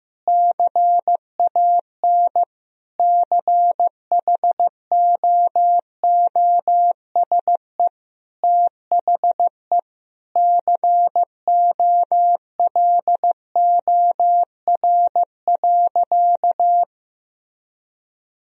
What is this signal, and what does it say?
Morse code, heard as a keyed tone, about 15 wpm, 700 Hz. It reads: CAN CHOOSE THE COLOR.
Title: Sentences from Top 1000 Words - Review Part 4 - 15wpm